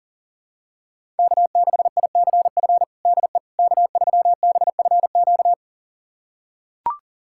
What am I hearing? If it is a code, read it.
Morse code, heard as a keyed tone, about 40 wpm, 700 Hz. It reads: K6ICF DE K3BFQ
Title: USA Callsign DE Callsign - 5 Characters - 40wpm